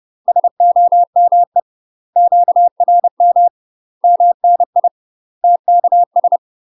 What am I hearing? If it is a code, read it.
Morse code, heard as a keyed tone, about 30 wpm, 700 Hz. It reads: SOME QRM MNI TKS